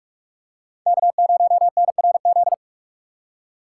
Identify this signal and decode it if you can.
Morse code, heard as a keyed tone, about 44 wpm, 700 Hz. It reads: K0NRZ